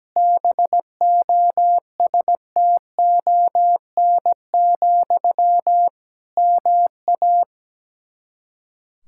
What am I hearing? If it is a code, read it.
Morse code, heard as a keyed tone, about 17 wpm, 700 Hz. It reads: BOSTON, MA
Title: Practice QSO - Exchange 1 - Beginning - 17wpm